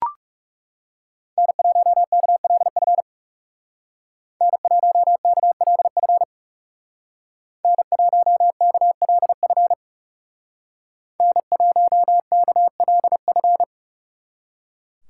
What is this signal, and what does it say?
Morse code, heard as a keyed tone, about 35 wpm, 700 Hz. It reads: N1KLF N1KLF N1KLF N1KLF
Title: Speed Racing - USA 1x3 Call Signs 30wpm